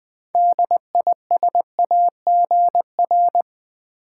Morse code, heard as a keyed tone, about 20 wpm, 700 Hz. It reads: DISAGR
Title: Build A Word - 6 Letters At A Time - 12-Letter Words - 20wpm